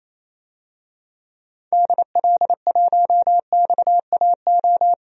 Morse code, heard as a keyed tone, about 28 wpm, 700 Hz. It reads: DL1XAO
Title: DXCC-100 Call Signs - 6 Characters - Repeated 2x - 28wpm